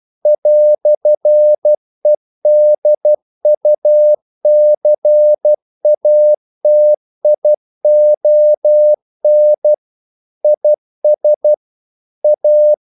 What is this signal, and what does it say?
Morse code, heard as a keyed tone, about 12 wpm, 600 Hz. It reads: "EDUCATION IS A
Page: English quotations in CW (telegraphy - morse)